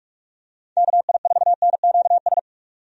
Morse code, heard as a keyed tone, about 46 wpm, 700 Hz. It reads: KI4NQS
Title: USA Call Signs - 6 Characters - 45wpm